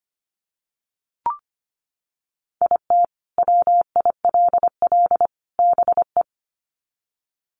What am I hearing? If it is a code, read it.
Morse code, heard as a keyed tone, about 25 wpm, 700 Hz. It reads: IT WILL BE
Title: Sets of 3 Words 25wpm